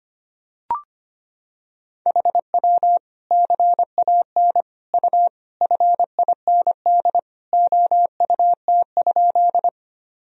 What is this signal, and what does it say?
Morse code, heard as a keyed tone, about 25 wpm, 700 Hz. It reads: HW CAN U FIND OUT?
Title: Sentences from Top 200 Words and Abbreviations 25wpm